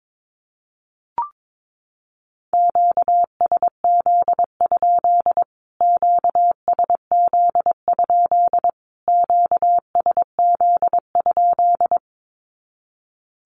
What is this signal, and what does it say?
Morse code, heard as a keyed tone, about 22 wpm, 700 Hz. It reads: QSZ? QSZ? QSZ?
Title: Q-Codes Repeated 3x 22wpm